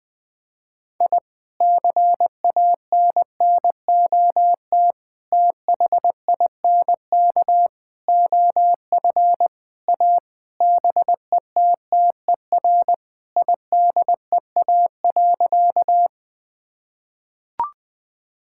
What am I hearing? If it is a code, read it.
Morse code, heard as a keyed tone, about 20 wpm, 700 Hz. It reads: I CANNOT THINK OF A BETTER IDEA.
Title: Sentences with 5-Word Phrases - 20wpm